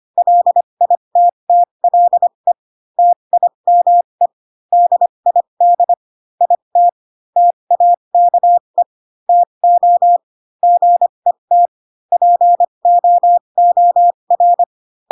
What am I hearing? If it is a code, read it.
Morse code, heard as a keyed tone, about 25 wpm, spaced as for 22 wpm, 700 Hz. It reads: LITTLE TIME DID IT TAKE TO GET POOR
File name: Pinocchio / Pinocchio ch4 18 25 ik1hge